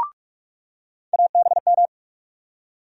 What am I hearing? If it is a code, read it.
Morse code, heard as a keyed tone, about 43 wpm, 700 Hz. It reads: ABM
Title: USA Call Sign Postfixes - 3 Letters 45wpm